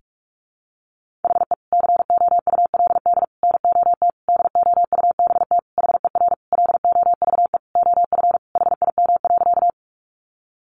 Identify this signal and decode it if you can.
Morse code, heard as a keyed tone, about 44 wpm, 700 Hz. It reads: HE COULD NOT DOUBT HER LOVE OF HIM.
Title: Sentences from Top 1500 Words 45wpm